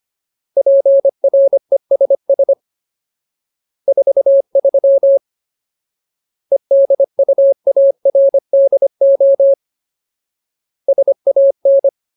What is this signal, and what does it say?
Morse code, heard as a keyed tone, about 25 wpm, 550 Hz. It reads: PRESS 43 EDUARDO SAN
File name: Ultimate Word List / 25WPM Ultimate